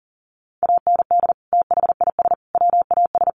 Morse code, heard as a keyed tone, about 40 wpm, 700 Hz. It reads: AND THIS WAS
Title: Sentences from Top 1100 Words 40wpm